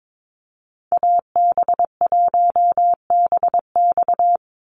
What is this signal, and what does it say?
Morse code, heard as a keyed tone, about 22 wpm, 700 Hz. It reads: AB1BX
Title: Call Signs Repeated 3x - 22wpm